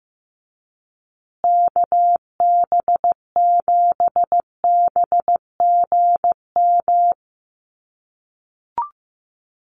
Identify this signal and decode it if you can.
Morse code, heard as a keyed tone, about 15 wpm, 700 Hz. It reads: KB7BGM
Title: USA Call Signs Repeated 3x 15wpm